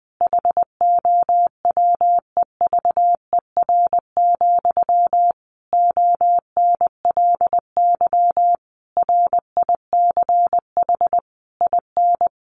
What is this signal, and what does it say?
Morse code, heard as a keyed tone, about 20 wpm, 700 Hz. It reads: HOWEVER, ONLY RICH IN
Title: Sentences from Top 1100 Words 20wpm